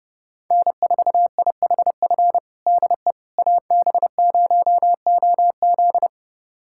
Morse code, heard as a keyed tone, about 32 wpm, 700 Hz. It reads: N4IHF DE AB0OZ
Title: USA Callsign DE Callsign - 5 Characters - 30wpm